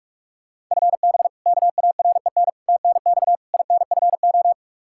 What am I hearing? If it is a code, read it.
Morse code, heard as a keyed tone, about 41 wpm, 700 Hz. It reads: FB KAREN TNX INFO
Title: K1USN SST QSO Practice 45wpm